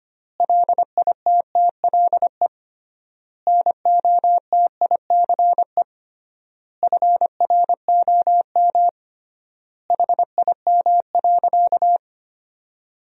Morse code, heard as a keyed tone, about 25 wpm, 700 Hz. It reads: LITTLE NOTICE FROM HIM.
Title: 3x Word-Spacing - Sentences from Top 400 Words - 25wpm